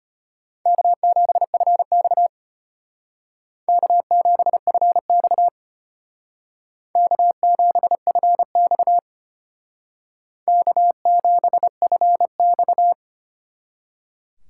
Morse code, heard as a keyed tone, about 32 wpm, 700 Hz. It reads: K7FX K7FX K7FX K7FX